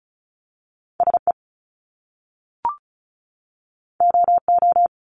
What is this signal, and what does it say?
Morse code, heard as a keyed tone, about 36 wpm, 700 Hz. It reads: SE OO
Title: Top Two Letter Combinations in Words 35wpm